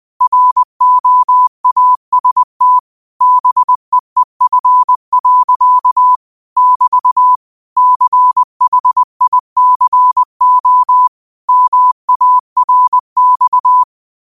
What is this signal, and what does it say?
Morse code, heard as a keyed tone, about 20 wpm, 1000 Hz. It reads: ROAST BEEF. <BT> CHICO MARX
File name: 20wpm